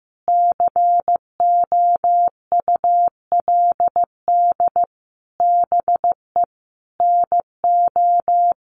Morse code, heard as a keyed tone, about 15 wpm, 700 Hz. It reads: COULD BE NO
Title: Sentences from Top 1500 Words 15wpm